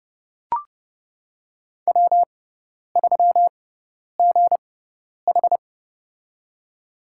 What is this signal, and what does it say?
Morse code, heard as a keyed tone, about 29 wpm, spaced as for 10 wpm, 700 Hz. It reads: W3GH
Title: USA Call Signs - 4 Characters - 6x Character Spacing - 30wpm